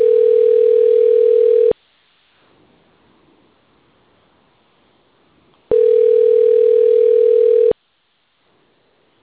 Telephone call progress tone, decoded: ringback tone

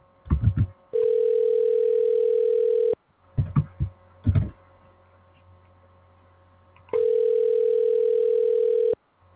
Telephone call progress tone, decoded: ringback tone

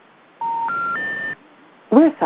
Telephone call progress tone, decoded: special information tone

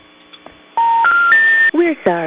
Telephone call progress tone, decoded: special information tone

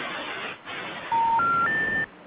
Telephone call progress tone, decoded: special information tone

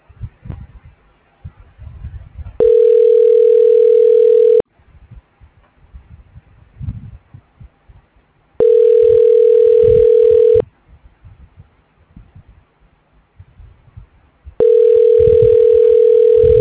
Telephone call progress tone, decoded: ringback tone

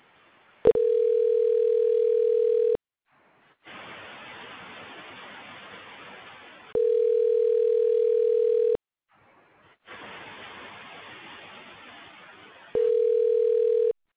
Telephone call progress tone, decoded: ringback tone

